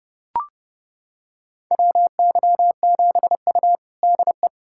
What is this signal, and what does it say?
Morse code, heard as a keyed tone, about 30 wpm, 700 Hz. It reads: WY7U DE
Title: USA Callsign DE Callsign - 4 Characters - 30wpm